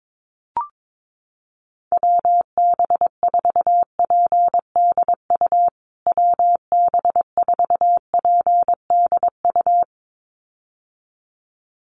Morse code, heard as a keyed tone, about 22 wpm, 700 Hz. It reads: WB4PDU WB4PDU
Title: USA Call Signs Repeated 2x 22wpm